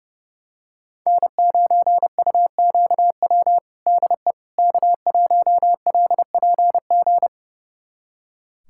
Morse code, heard as a keyed tone, about 30 wpm, 700 Hz. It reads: N9UQW DE K1LPG
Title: USA Callsign DE Callsign - 5 Characters - 30wpm